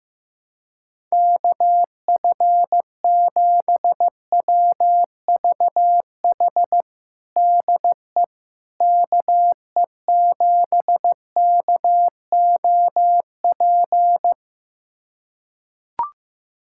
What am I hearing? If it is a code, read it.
Morse code, heard as a keyed tone, about 15 wpm, 700 Hz. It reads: KF7WVH DE KE7KOP
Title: USA Callsign DE Callsign - 6 Characters - 15wpm